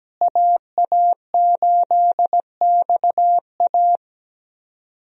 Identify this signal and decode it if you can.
Morse code, heard as a keyed tone, about 17 wpm, 700 Hz. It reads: AA8XA
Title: USA Call Signs - 5 Characters - Repeated 3x - 17wpm